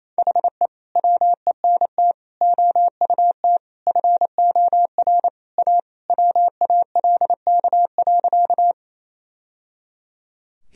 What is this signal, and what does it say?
Morse code, heard as a keyed tone, about 28 wpm, 700 Hz. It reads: HE WENT OUT FOR A WALK.
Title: Sentences from Top 1000 Words - Review Part 1 - 28wpm